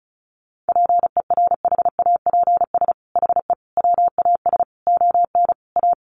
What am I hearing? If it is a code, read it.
Morse code, heard as a keyed tone, about 34 wpm, 700 Hz. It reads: PERHAPS HE WAS ON A
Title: Sentences from Top 1100 Words 35wpm